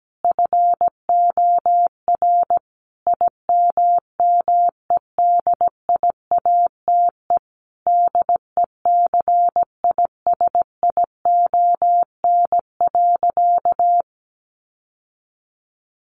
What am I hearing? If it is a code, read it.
Morse code, heard as a keyed tone, about 17 wpm, 700 Hz. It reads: FOR IMMEDIATE DECISION.
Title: Sentences from Top 1200 Words 17wpm